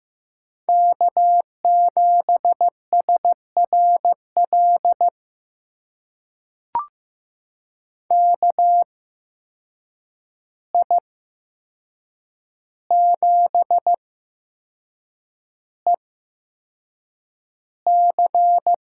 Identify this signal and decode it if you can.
Morse code, heard as a keyed tone, about 15 wpm, spaced as for 6 wpm, 700 Hz. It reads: K7SRL KI7EC
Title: USA Call Signs - 5 Characters - 8x Character Spacing - 15wpm